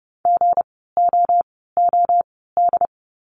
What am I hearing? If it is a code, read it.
Morse code, heard as a keyed tone, about 30 wpm, spaced as for 17 wpm, 700 Hz. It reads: GOOD
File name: course / 30:10wpm / 230 Lesson 37f - Letter x - Words 30wpm